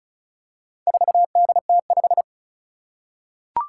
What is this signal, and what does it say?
Morse code, heard as a keyed tone, about 36 wpm, 690 Hz. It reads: 4DT5
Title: Four Challenging Letter-Number Combinations 35wpm